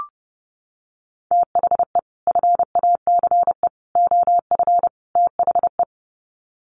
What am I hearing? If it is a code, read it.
Morse code, heard as a keyed tone, about 30 wpm, 700 Hz. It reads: THE FACE OF THE
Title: Sets of 4 Words 30wpm